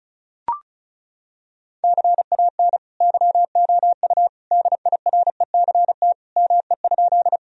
Morse code, heard as a keyed tone, about 36 wpm, 700 Hz. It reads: CAN YOU DIRECT ME?
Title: Sentences from Top 1000 Words - Review Part 2 - 35wpm